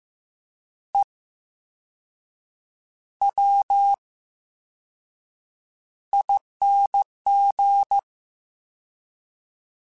Morse code, heard as a keyed tone, about 15 wpm, spaced as for 6 wpm, 800 Hz. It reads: EWING
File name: Special INGs / Special INGs-mp3s / Slow ing @13